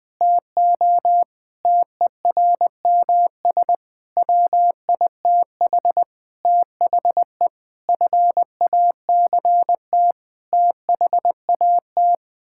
Morse code, heard as a keyed tone, about 20 wpm, 700 Hz. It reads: TO TERMS WITH THE FACT THAT
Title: Sets of 8 Words - 20wpm